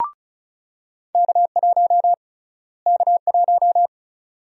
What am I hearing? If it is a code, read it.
Morse code, heard as a keyed tone, about 34 wpm, 700 Hz. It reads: K1 K1
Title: USA Call Sign Prefixes - Repeated 2x - 35wpm